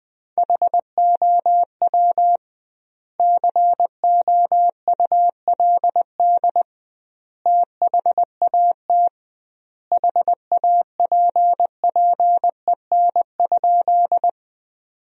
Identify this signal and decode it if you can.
Morse code, heard as a keyed tone, about 20 wpm, 700 Hz. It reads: HOW COULD THAT HAPPEN?
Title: 2x Word-Spacing - Sentences from Top 400 Words - 20wpm